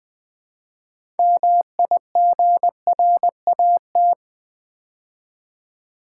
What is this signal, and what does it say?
Morse code, heard as a keyed tone, about 20 wpm, 700 Hz. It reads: MIGRAT
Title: Build A Word - 3 Letters At A Time - 9-Letter Words - 20wpm